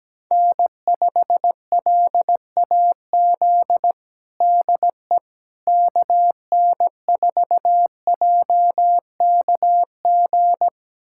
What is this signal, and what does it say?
Morse code, heard as a keyed tone, about 17 wpm, 700 Hz. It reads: N5LAZ DE KN4JKG
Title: USA Callsign DE Callsign 17wpm